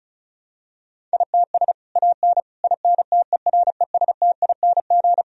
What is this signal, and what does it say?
Morse code, heard as a keyed tone, about 36 wpm, 700 Hz. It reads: ITS AN INTERESTING
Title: Sentences from Top 1000 Words - Review Part 4 - 35wpm